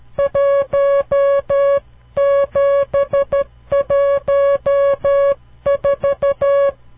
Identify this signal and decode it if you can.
Morse code, heard as a keyed tone, about 12 wpm, 560 Hz. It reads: 1714
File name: CW Learning Programs / farnsworth / unit 5